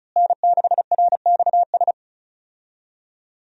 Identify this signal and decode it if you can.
Morse code, heard as a keyed tone, about 36 wpm, 700 Hz. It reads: N6RXS
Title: USA Call Signs - 5 Characters - Repeated 3x - 35wpm